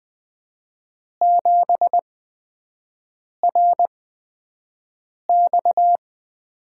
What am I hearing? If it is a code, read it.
Morse code, heard as a keyed tone, about 20 wpm, spaced as for 5 wpm, 700 Hz. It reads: 7RX
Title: USA Call Signs - 5 Characters - 8x Character Spacing - 20wpm